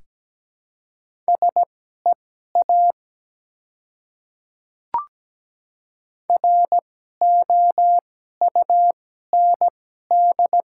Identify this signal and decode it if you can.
Morse code, heard as a keyed tone, about 17 wpm, spaced as for 12 wpm, 700 Hz. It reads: SEA ROUND
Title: 2x Character Spacing - Top 300 Words - 17wpm